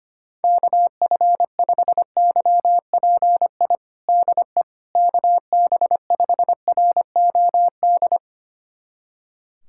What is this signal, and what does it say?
Morse code, heard as a keyed tone, about 25 wpm, 700 Hz. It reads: KF5YPI DE KB5ROD
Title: USA Callsign DE Callsign 25wpm